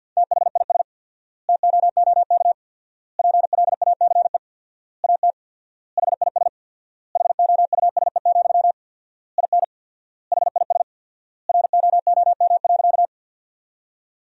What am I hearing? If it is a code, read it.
Morse code, heard as a keyed tone, about 50 wpm, 700 Hz. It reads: THIS TOOK PLACE AT HIS HOUSE, IN HIS ROOM.
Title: 4x Word-Spacing - Sentences from Top 300 Words - 50wpm